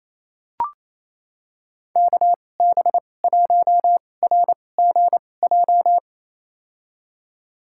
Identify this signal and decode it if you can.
Morse code, heard as a keyed tone, about 28 wpm, spaced as for 20 wpm, 700 Hz. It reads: KB1RGJ
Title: USA Call Signs - 6 Characters - 2x Character Spacing - 28wpm